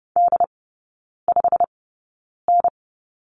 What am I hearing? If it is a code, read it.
Morse code, heard as a keyed tone, about 30 wpm, spaced as for 9 wpm, 690 Hz. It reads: D5N
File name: Lesson 16e - Letter c - Three Characters 30wpm